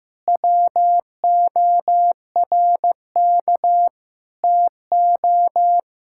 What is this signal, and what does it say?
Morse code, heard as a keyed tone, about 15 wpm, 700 Hz. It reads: WORK TO
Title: Sentences from Top 1000 Words - Review Part 5 - 15wpm